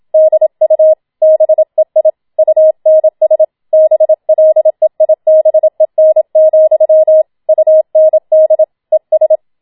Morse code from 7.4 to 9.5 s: UND ES